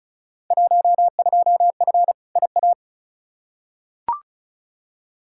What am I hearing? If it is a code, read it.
Morse code, heard as a keyed tone, about 36 wpm, 700 Hz. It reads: 12F IA
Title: Field Day Exchanges Encore - 35wpm